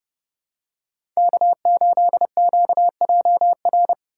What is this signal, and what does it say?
Morse code, heard as a keyed tone, about 30 wpm, 700 Hz. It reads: K8QJR
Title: USA Call Signs - 5 Characters - 8x Character Spacing - 30wpm